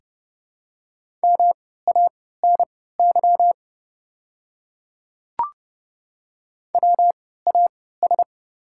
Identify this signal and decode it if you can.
Morse code, heard as a keyed tone, about 30 wpm, spaced as for 17 wpm, 700 Hz. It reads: MANY WAS
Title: 3x Character Spacing - Top 100 Words - 30wpm